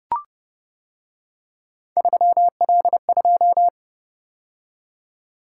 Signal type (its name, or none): Morse code